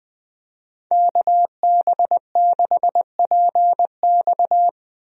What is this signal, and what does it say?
Morse code, heard as a keyed tone, about 21 wpm, 700 Hz. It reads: KB6PX
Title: USA Call Signs - 5 Characters - Repeated 3x - 20wpm